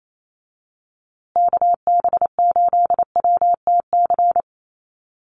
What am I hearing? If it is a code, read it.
Morse code, heard as a keyed tone, about 28 wpm, 700 Hz. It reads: KB8WTC
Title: USA Call Signs Repeated 2x 28wpm